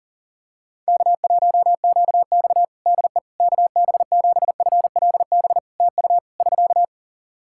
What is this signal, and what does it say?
Morse code, heard as a keyed tone, about 40 wpm, 700 Hz. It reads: K1QX DE KB7FLB TU <SK>
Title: Practice QSO - Exchange 4 - Ending - 40wpm